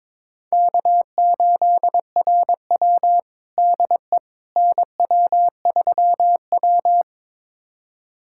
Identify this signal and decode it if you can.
Morse code, heard as a keyed tone, about 22 wpm, 700 Hz. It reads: K8RW DE NW3W
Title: USA Callsign DE Callsign - 4 Characters - 22wpm